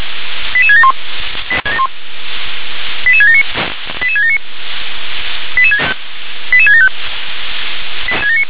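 Five-tone selective call (ZVEI), heard as quick runs of five tones, tones of about 70 ms. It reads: 88671 88679 88679 88675
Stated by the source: ZVEI five-tone selective call 88776 received with an SDR.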